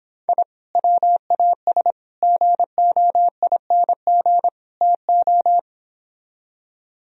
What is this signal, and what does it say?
Morse code, heard as a keyed tone, about 31 wpm, 700 Hz. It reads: I WAS GOING TO